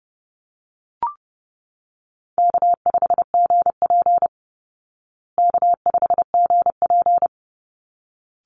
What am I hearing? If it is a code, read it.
Morse code, heard as a keyed tone, about 30 wpm, 700 Hz. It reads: K5GP K5GP
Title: Easy Call Signs Repeated 2x - 30wpm